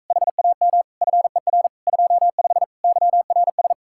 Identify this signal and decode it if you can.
Morse code, heard as a keyed tone, about 44 wpm, 700 Hz. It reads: HAM FER 25 YRS